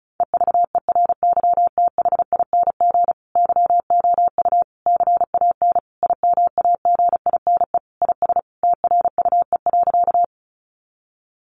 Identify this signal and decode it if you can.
Morse code, heard as a keyed tone, about 34 wpm, 700 Hz. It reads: EVERYTHING YOU CAN IMAGINE IS TRUE.
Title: Sentences from Top 1400 Words 35wpm